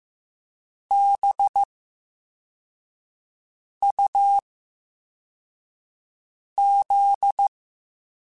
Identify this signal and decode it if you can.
Morse code, heard as a keyed tone, about 15 wpm, 800 Hz. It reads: B U Z
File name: Special INGs-mp3s / Special INGs-mp3s / VerySlowToFaster Ing @13